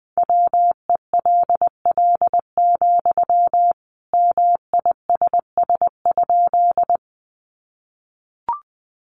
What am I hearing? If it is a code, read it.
Morse code, heard as a keyed tone, about 20 wpm, 700 Hz. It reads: WELL, MISS?
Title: Sentences from Top 500 Words 20wpm